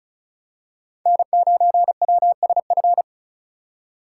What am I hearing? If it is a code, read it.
Morse code, heard as a keyed tone, about 34 wpm, 700 Hz. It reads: N9WSF
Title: USA Call Signs - 5 Characters - 3x Character Spacing - 35wpm